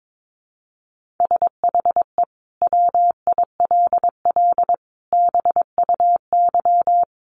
Morse code, heard as a keyed tone, about 22 wpm, 700 Hz. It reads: SHE WILL BUY
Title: Sentences from Top 1500 Words 22wpm